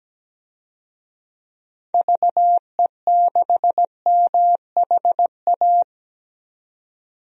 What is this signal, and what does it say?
Morse code, heard as a keyed tone, about 17 wpm, 700 Hz. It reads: VE6MHA
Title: DXCC-100 Call Signs - 6 Characters - 17wpm